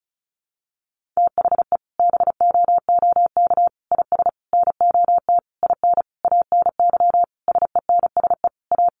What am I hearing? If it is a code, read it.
Morse code, heard as a keyed tone, about 34 wpm, 700 Hz. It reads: THE BOOK IS NOT IN ANY SENSE A
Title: Sentences from Top 1300 Words 35wpm